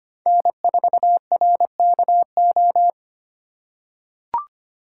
Morse code, heard as a keyed tone, about 25 wpm, 700 Hz. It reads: N4RKO
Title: USA Call Signs - 5 Characters - 6x Character Spacing - 25wpm